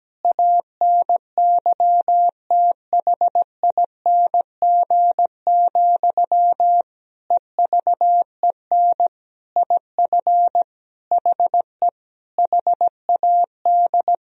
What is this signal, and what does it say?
Morse code, heard as a keyed tone, about 17 wpm, 700 Hz. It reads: ANYTHING, EVEN IF HE HAD
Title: Sentences from Top 1000 Words - Review Part 1 - 17wpm